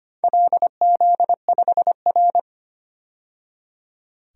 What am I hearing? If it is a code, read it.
Morse code, heard as a keyed tone, about 25 wpm, 700 Hz. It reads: LZ5R
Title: DXCC-100 Call Signs - 4 Characters - 25wpm